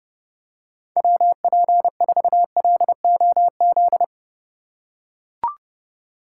Morse code, heard as a keyed tone, about 30 wpm, 700 Hz. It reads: WP4LOZ
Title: USA Call Signs - 6x Character Spacing - 30wpm